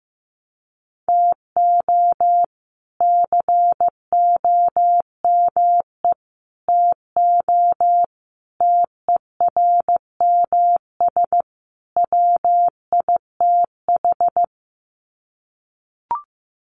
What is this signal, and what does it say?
Morse code, heard as a keyed tone, about 15 wpm, 700 Hz. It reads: TO COME TO TERMS WITH
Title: Sets of 5 Words Encore 15wpm